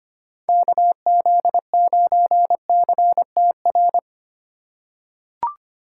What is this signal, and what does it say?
Morse code, heard as a keyed tone, about 25 wpm, 700 Hz. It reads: KZ9CTR
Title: USA Call Signs - 6 Characters - 2x Character Spacing - 25wpm